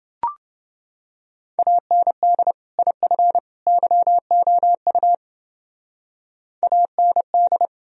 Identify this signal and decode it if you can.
Morse code, heard as a keyed tone, about 31 wpm, 700 Hz. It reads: AND IF YOU AND